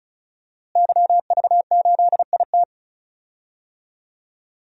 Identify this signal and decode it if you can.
Morse code, heard as a keyed tone, about 34 wpm, 700 Hz. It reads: YV8IT
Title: DXCC-100 Call Signs - 5 Characters - Repeated 3x - 35wpm